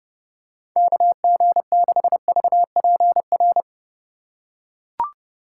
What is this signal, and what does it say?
Morse code, heard as a keyed tone, about 31 wpm, 700 Hz. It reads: KG6VPR